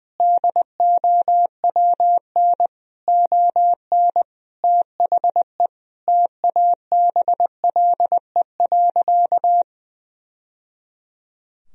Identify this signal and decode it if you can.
Morse code, heard as a keyed tone, about 20 wpm, 700 Hz. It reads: DOWN ON THE TABLE.
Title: Sentences from Top 1000 Words - Review Part 3 - 20wpm